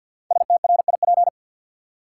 Morse code, heard as a keyed tone, about 53 wpm, 700 Hz. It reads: STRIP
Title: Sentences from Top 1600 Words - New Word Drill - 50wpm